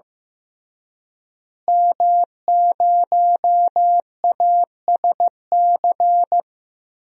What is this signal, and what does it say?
Morse code, heard as a keyed tone, about 15 wpm, 700 Hz. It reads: M0ASC
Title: DXCC-100 Call Signs - 5 Characters - Repeated 2x - 15wpm